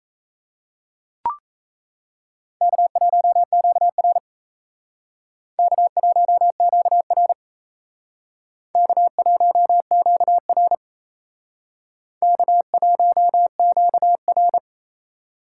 Morse code, heard as a keyed tone, about 38 wpm, 700 Hz. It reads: K1QR K1QR K1QR K1QR